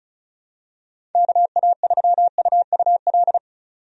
Morse code, heard as a keyed tone, about 34 wpm, 700 Hz. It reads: KA3UUL
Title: USA Call Signs - 6 Characters - 35wpm